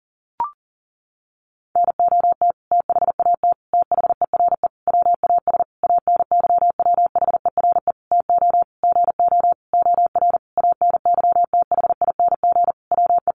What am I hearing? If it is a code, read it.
Morse code, heard as a keyed tone, about 40 wpm, 700 Hz. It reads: NOT THAT THERE WAS ANYWHERE TO GO OR ANYTHING WE
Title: Sentences from Top 1400 Words 40wpm